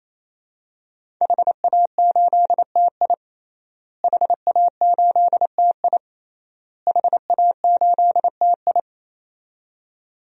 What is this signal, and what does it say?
Morse code, heard as a keyed tone, about 27 wpm, 700 Hz. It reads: HA8TI HA8TI HA8TI
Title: DXCC-100 Call Signs - 5 Characters - Repeated 3x - 28wpm